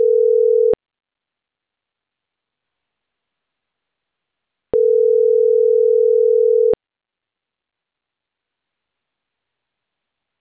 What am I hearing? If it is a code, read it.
Ringback tone (North America), 440 + 480 Hz.